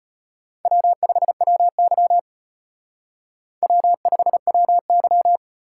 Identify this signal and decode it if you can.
Morse code, heard as a keyed tone, about 30 wpm, 700 Hz. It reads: W5WY W5WY
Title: Speed Racing - USA 1x2 Call Signs 25wpm